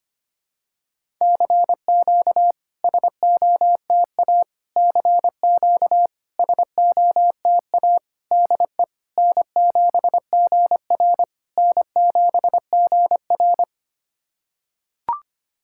Morse code, heard as a keyed tone, about 25 wpm, 690 Hz. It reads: CQ SOTA CQ SOTA DE N7GR N7GR
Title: SOTA QSO Practice 25wpm